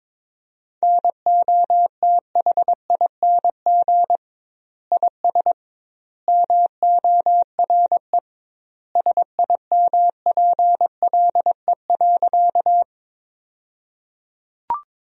Morse code, heard as a keyed tone, about 21 wpm, 700 Hz. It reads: NOTHING IS MORE SIMPLE.
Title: 2x Word-Spacing - Sentences from Top 500 Words - 22wpm